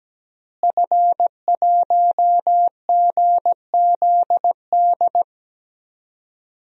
Morse code, heard as a keyed tone, about 17 wpm, 700 Hz. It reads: F1GZD
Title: DXCC-100 Call Signs - 5 Characters - Repeated 2x - 17wpm